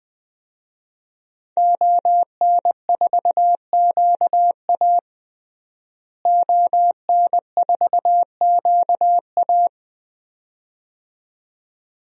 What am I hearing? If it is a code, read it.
Morse code, heard as a keyed tone, about 21 wpm, 700 Hz. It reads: ON4QA ON4QA
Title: DXCC-100 Call Signs - 5 Characters - Repeated 2x - 20wpm